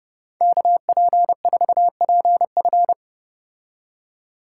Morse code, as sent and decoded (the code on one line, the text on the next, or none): -.- .--. ....- .--. ..-.
KP4PF